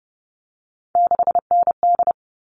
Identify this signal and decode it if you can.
Morse code, heard as a keyed tone, about 30 wpm, 700 Hz. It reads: 6ND
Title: All Cut Numbers - 3-Digit 30wpm